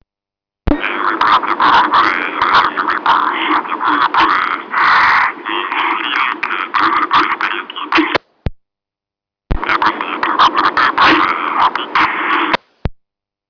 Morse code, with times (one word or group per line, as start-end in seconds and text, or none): none